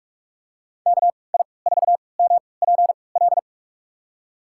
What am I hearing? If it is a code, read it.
Morse code, heard as a keyed tone, about 46 wpm, spaced as for 26 wpm, 700 Hz. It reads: KI4MPL
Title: USA Call Signs - 6 Characters - 3x Character Spacing - 45wpm